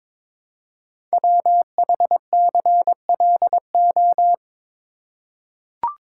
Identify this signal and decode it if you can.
Morse code, heard as a keyed tone, about 22 wpm, 690 Hz. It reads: WHCLO
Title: Code Groups - 5 Letter:Numbers - 22wpm